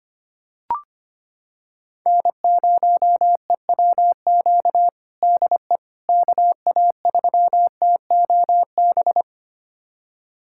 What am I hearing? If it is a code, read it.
Morse code, heard as a keyed tone, about 25 wpm, 700 Hz. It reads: N0EWQ DE KA3TOB
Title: USA Callsign DE Callsign 25wpm